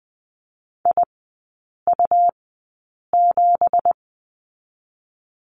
Morse code, heard as a keyed tone, about 19 wpm, spaced as for 8 wpm, 690 Hz. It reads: IU7
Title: 166 Lesson 27e - Number 7 - Three Characters 20wpm